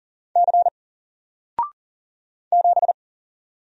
Morse code, heard as a keyed tone, about 39 wpm, 700 Hz. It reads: C 7
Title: Single Character (Letter:Number:Punctuation:Prosign) Alt 40wpm